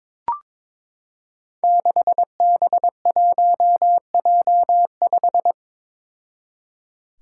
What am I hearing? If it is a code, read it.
Morse code, heard as a keyed tone, about 23 wpm, 700 Hz. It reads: 6B1J5
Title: Five Challenging Letter-Number Combinations 22wpm